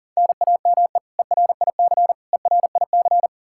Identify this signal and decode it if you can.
Morse code, heard as a keyed tone, about 40 wpm, 700 Hz. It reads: NAME ERIC ERIC
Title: Practice QSO - Exchange 1 - Beginning - 40wpm